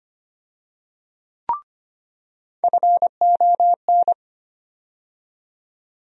Morse code, heard as a keyed tone, about 25 wpm, 700 Hz. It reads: FON
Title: USA Call Sign Postfixes - 3 Letters 25wpm